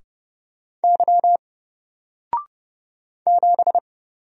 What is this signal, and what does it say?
Morse code, heard as a keyed tone, about 30 wpm, 700 Hz. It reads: Y 7
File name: Single-Letter-Number-30wpm